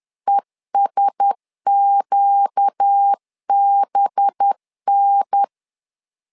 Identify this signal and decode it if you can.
Morse code, heard as a keyed tone, about 10 wpm, 800 Hz. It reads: ESQBN
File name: LEZIONI / 5 LBVC / 5 LBV 10wpm 2248